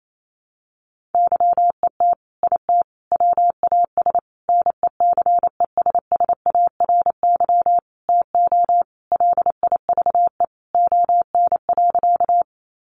Morse code, heard as a keyed tone, about 28 wpm, 700 Hz. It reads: YET IT WAS NECESSARY TO LIVE ON.
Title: Sentences from Top 800 Words 28wpm